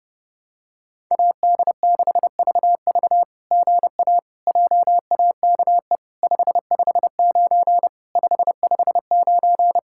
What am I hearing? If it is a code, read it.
Morse code, heard as a keyed tone, about 30 wpm, 700 Hz. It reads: AD6VV GA JAKE 559 559
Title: SOTA QSO Practice 30wpm